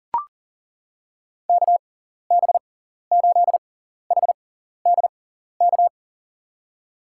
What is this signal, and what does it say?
Morse code, heard as a keyed tone, about 37 wpm, spaced as for 14 wpm, 700 Hz. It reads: KB8HDK
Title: USA Call Signs - 6x Character Spacing - 40wpm